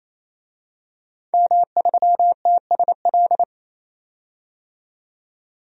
Morse code, heard as a keyed tone, about 27 wpm, 700 Hz. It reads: M3TSL
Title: DXCC-100 Call Signs - 5 Characters - 28wpm